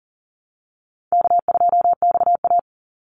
Morse code, heard as a keyed tone, about 40 wpm, 700 Hz. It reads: K2XA
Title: Call Signs - 40wpm